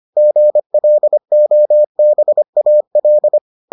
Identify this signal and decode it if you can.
Morse code, heard as a keyed tone, about 25 wpm, 600 Hz. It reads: GLOBAL